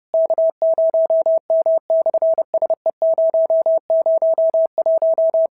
Morse code, heard as a keyed tone, about 28 wpm, 650 Hz. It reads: K0M/SE001
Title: SOTA QSO Practice 30wpm